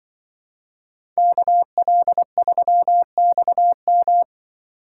Morse code, heard as a keyed tone, about 25 wpm, 700 Hz. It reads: KL3XM
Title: Speed Racing - USA Call Signs 20wpm